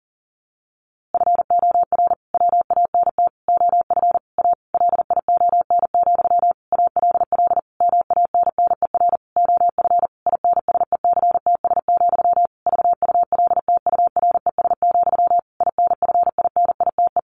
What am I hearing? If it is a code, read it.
Morse code, heard as a keyed tone, about 40 wpm, 700 Hz. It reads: FOR WANT OF A LION, ALL MANNER OF INSECTS, VULTURES, INFINITE
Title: A Sound Of Thunder 40wpm